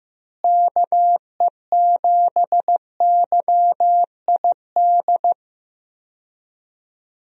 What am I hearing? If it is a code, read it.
Morse code, heard as a keyed tone, about 15 wpm, 700 Hz. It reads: KE7YID
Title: USA Call Signs - 6 Characters - Repeated 3x - 15wpm